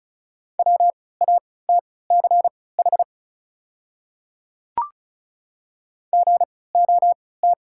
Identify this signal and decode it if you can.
Morse code, heard as a keyed tone, about 34 wpm, spaced as for 20 wpm, 690 Hz. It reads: WATCH GOT
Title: 3x Character Spacing - Top 300 Words - 35wpm